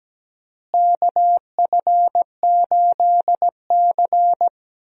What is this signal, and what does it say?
Morse code, heard as a keyed tone, about 17 wpm, 700 Hz. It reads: KF8C
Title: USA Call Signs - 4 Characters - Repeated 3x - 17wpm